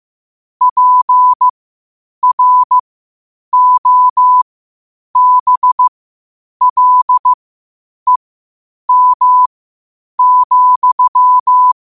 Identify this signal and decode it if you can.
Morse code, heard as a keyed tone, about 15 wpm, spaced as for 8 wpm, 1000 Hz. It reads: PROBLEM,
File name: 8wpm